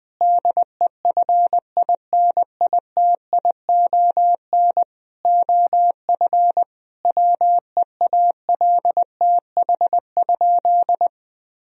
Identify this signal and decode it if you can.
Morse code, heard as a keyed tone, about 21 wpm, 700 Hz. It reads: DEFINITION OF WEALTH?
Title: Sentences from Top 1000 Words - Review Part 4 - 20wpm